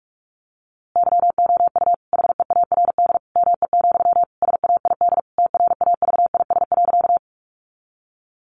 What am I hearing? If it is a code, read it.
Morse code, heard as a keyed tone, about 44 wpm, 700 Hz. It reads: YOU HEARD ME, SAID TRAVIS.
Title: A Sound Of Thunder 45wpm